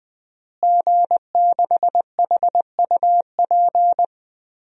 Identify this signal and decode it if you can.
Morse code, heard as a keyed tone, about 20 wpm, 700 Hz. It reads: G6HUP
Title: DXCC-100 Call Signs - 5 Characters - Repeated 2x - 20wpm